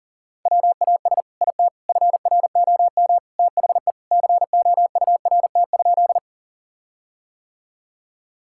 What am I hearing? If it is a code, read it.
Morse code, heard as a keyed tone, about 37 wpm, 700 Hz. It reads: WAS IT FROM THE COURT?
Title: Sentences from Top 1600 Words 40wpm